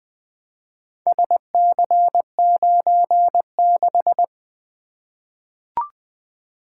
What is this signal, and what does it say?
Morse code, heard as a keyed tone, about 20 wpm, 700 Hz. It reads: SC96
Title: Four Letter-Number Combinations - 20wpm